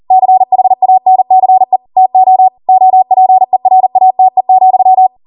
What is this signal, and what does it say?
Morse code, heard as a keyed tone, about 40 wpm, 750 Hz. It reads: CHANCE TO OPERATE,